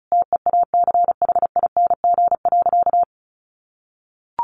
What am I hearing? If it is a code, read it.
Morse code, heard as a keyed tone, about 34 wpm, 700 Hz. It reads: TEACHING.
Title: Sentences from Top 1100 Words 35wpm